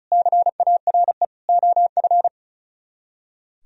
Morse code, heard as a keyed tone, about 46 wpm, 700 Hz. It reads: CARE OF